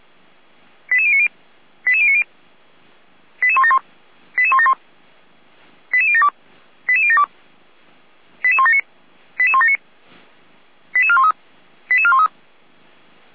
Five-tone selective call (ZVEI), heard as quick runs of five tones, tones of about 70 ms. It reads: 88090 88090 88171 88171 88072 88072 88179 88179 88413 88413